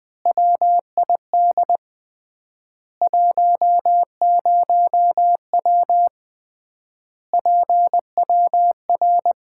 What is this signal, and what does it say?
Morse code, heard as a keyed tone, about 20 wpm, 700 Hz. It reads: WID 10W PWR
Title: Common QSO Phrases - 3x Word-Spacing - 20wpm